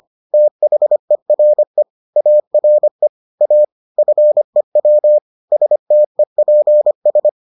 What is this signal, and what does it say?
Morse code, heard as a keyed tone, about 25 wpm, 600 Hz. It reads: THERE ARE A FEW STEPS